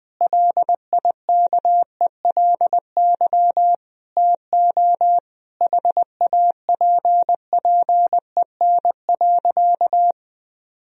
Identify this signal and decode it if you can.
Morse code, heard as a keyed tone, about 20 wpm, 700 Hz. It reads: LIKELY TO HAPPEN.
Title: Sentences from Top 1000 Words - Review Part 4 - 20wpm